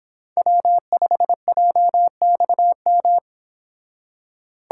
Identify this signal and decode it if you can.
Morse code, heard as a keyed tone, about 28 wpm, 700 Hz. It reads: W5JXM